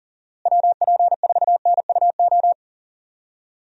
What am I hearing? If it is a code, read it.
Morse code, heard as a keyed tone, about 40 wpm, 700 Hz. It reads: WP4NUO
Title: USA Call Signs - 6 Characters - Repeated 3x - 40wpm